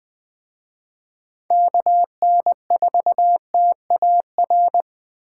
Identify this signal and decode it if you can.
Morse code, heard as a keyed tone, about 21 wpm, 700 Hz. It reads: KN4TAR